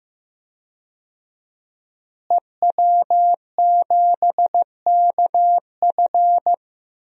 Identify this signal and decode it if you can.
Morse code, heard as a keyed tone, about 15 wpm, 700 Hz. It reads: EW7KF
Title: DXCC-100 Call Signs - 5 Characters - Repeated 2x - 15wpm